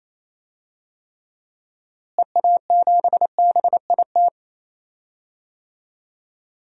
Morse code, heard as a keyed tone, about 28 wpm, 700 Hz. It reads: EA7BIT
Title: DXCC-100 Call Signs - 6 Characters - Repeated 2x - 28wpm